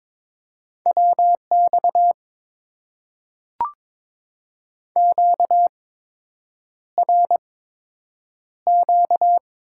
Morse code, heard as a keyed tone, about 22 wpm, spaced as for 8 wpm, 700 Hz. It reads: WX QRQ
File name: Top 90 QSO Elements - 8x Character Spacing - 22wpm